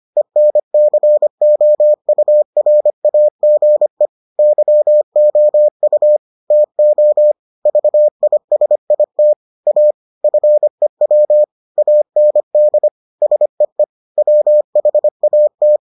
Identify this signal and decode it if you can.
Morse code, heard as a keyed tone, about 25 wpm, 600 Hz. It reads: ENCOURAGE YOU TO VISIT A FEW AND SEE WHAT